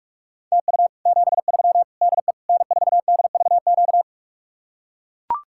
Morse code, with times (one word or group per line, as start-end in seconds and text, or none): 0.5-0.9 TU
1.1-1.8 73
2.0-2.3 DE
2.5-4.0 N4DVQ